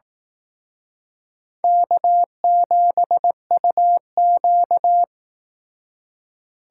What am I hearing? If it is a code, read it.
Morse code, heard as a keyed tone, about 19 wpm, 700 Hz. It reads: K7UQ